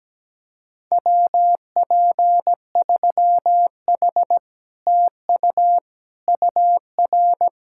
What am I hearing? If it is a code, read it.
Morse code, heard as a keyed tone, about 17 wpm, 700 Hz. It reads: WP3H TU UR
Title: Speed-Racing Common QSO Phrases - 17wpm